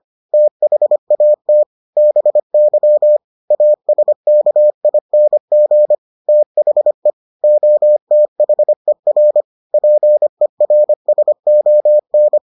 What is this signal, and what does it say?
Morse code, heard as a keyed tone, about 25 wpm, 600 Hz. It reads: THAT BY ASKING THE OTHER PERSON